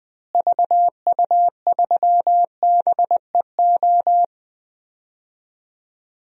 Morse code, heard as a keyed tone, about 20 wpm, 700 Hz. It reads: VU3BEO
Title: DXCC-100 Call Signs - 6 Characters - Repeated 2x - 20wpm